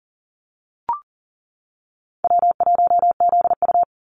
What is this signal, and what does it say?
Morse code, heard as a keyed tone, about 40 wpm, 700 Hz. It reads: W1ZU
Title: NA SOTA Activator Callsigns 40wpm